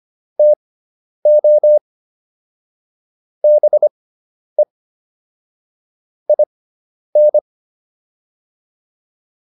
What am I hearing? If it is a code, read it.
Morse code, heard as a keyed tone, about 25 wpm, spaced as for 10 wpm, 600 Hz. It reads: TO BE IN